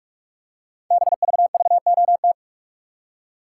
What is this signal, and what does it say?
Morse code, heard as a keyed tone, about 44 wpm, 700 Hz. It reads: BVVOT
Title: Code Groups - 5 Letter:Numbers - 45wpm